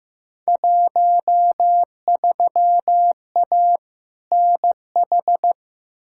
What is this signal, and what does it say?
Morse code, heard as a keyed tone, about 15 wpm, 700 Hz. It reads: 13A NH
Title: Field Day Exchanges Encore - 15wpm